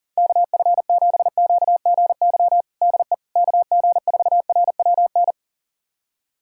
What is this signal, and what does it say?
Morse code, heard as a keyed tone, about 40 wpm, 700 Hz. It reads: KF7QGY DE KG4RWN
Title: USA Callsign DE Callsign - 6 Characters - 40wpm